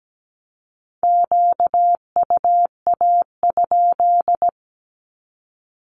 Morse code, heard as a keyed tone, about 17 wpm, 700 Hz. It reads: QUA?
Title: Q-Codes 17wpm